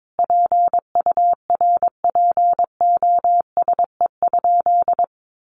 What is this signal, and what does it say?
Morse code, heard as a keyed tone, about 22 wpm, 700 Hz. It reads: PURPOSE?
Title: Sentences from Top 1200 Words 22wpm